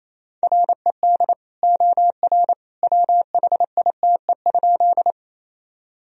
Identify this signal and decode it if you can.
Morse code, heard as a keyed tone, about 28 wpm, 700 Hz. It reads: RED OR WHITE?
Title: Sentences from Top 1000 Words - Review Part 2 - 28wpm